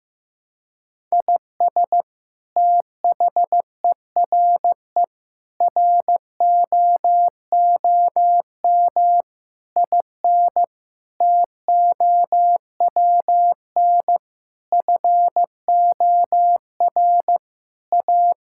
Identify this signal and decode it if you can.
Morse code, heard as a keyed tone, about 15 wpm, 700 Hz. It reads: IS THERE ROOM IN TOWN FOR A
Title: Sentences from Top 1000 Words - Review Part 5 - 15wpm